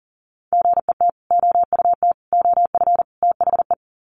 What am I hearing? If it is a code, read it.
Morse code, heard as a keyed tone, about 40 wpm, 700 Hz. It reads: GET OUT OF THE
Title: Sets of 4 Words 40wpm